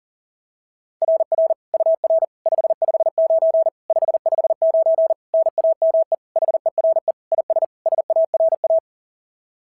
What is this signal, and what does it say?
Morse code, heard as a keyed tone, about 40 wpm, 690 Hz. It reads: RR UR 559 559 NAME HERE IS SARA
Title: SOTA QSO Practice 40wpm